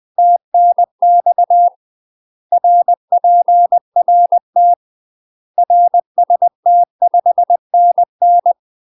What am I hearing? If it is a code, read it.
Morse code, heard as a keyed tone, about 20 wpm, 700 Hz. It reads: TNX RPRT RST 5NN
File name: UK MASTER QSOs TRAINING / 20WPM / UK MASTER QSOs 0028